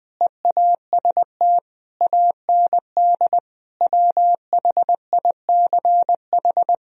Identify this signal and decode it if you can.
Morse code, heard as a keyed tone, about 20 wpm, 700 Hz. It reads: EAST AND WHICH
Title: Sentences from Top 1000 Words - Review Part 3 - 20wpm